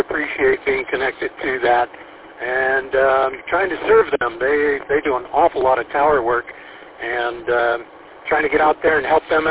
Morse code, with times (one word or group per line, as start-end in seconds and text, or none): none